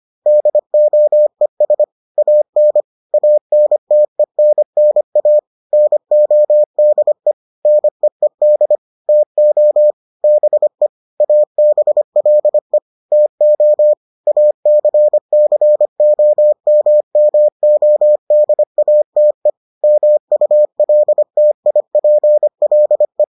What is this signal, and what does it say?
Morse code, heard as a keyed tone, about 25 wpm, 600 Hz. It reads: DOES AN ANTENNA NODE NEED TO BE ABLE TO ACCOMMODATE MULTIPLE